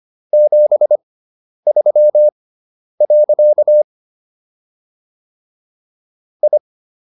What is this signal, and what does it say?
Morse code, heard as a keyed tone, about 25 wpm, spaced as for 10 wpm, 600 Hz. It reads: 73. I